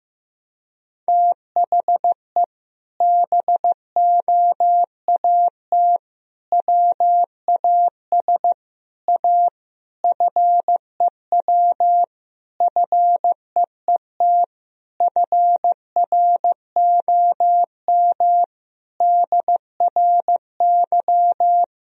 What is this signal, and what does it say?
Morse code, heard as a keyed tone, about 15 wpm, 700 Hz. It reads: THE BOAT WAS A FEW FEET FROM DRY
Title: Sentences from Top 1000 Words - Review Part 3 - 15wpm